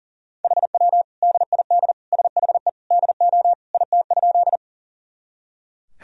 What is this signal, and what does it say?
Morse code, heard as a keyed tone, about 40 wpm, 700 Hz. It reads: HW DID SHE DO IT?
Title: Sentences from Top 100 Words and Abbreviations 40wpm